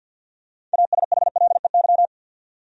Morse code, heard as a keyed tone, about 50 wpm, 700 Hz. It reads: ASHLEY
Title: USA Common Names - 50wpm